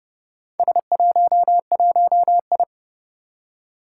Morse code, heard as a keyed tone, about 30 wpm, 700 Hz. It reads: S11I
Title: Four Challenging Letter-Number Combinations 30wpm